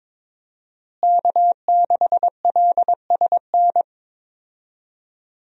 Morse code, heard as a keyed tone, about 22 wpm, 700 Hz. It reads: K6LSN
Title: USA Call Signs - 5 Characters - Repeated 2x - 22wpm